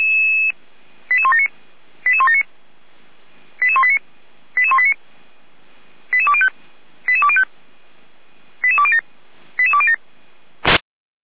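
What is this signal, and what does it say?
Five-tone selective call (ZVEI), heard as quick runs of five tones, tones of about 70 ms. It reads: 88179 88179 88189 88189 88205 88205 88207 88207
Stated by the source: ZVEI five-tone selective call received by SDR, fire brigade Feuerwehr Orlamünde.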